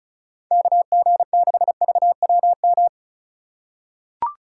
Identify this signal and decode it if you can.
Morse code, heard as a keyed tone, about 34 wpm, 700 Hz. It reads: KG6VWM
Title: USA Call Signs - 6 Characters - 2x Character Spacing - 35wpm